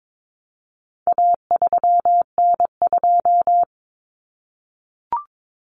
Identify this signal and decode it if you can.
Morse code, heard as a keyed tone, about 22 wpm, 700 Hz. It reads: A3N2
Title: Common Cut Numbers - 4-Digit 22wpm